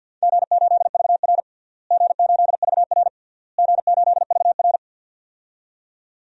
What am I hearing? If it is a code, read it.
Morse code, heard as a keyed tone, about 50 wpm, 700 Hz. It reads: G8VR G8VR G8VR
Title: DXCC-100 Call Signs - 4 Characters - Repeated 3x - 50wpm